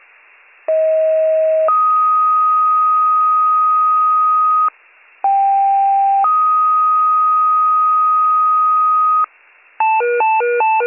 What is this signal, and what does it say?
Two-tone page, heard as a one-second tone, then a three-second tone, about 640 then 1250 Hz, about 790 then 1250 Hz. It is heard from 0.7 to 9.2 s.